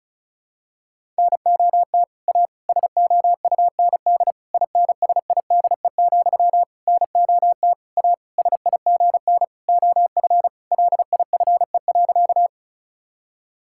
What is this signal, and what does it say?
Morse code, heard as a keyed tone, about 36 wpm, 700 Hz. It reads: NOT A SOUND INSIDE, NOT A SIGN OF LIFE.
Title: Sentences from Top 1000 Words - Review Part 6 - 35wpm